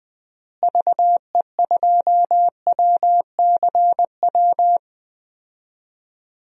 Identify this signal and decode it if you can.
Morse code, heard as a keyed tone, about 21 wpm, 700 Hz. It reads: VE2WCW
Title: DXCC-100 Call Signs - 6 Characters - Repeated 3x - 20wpm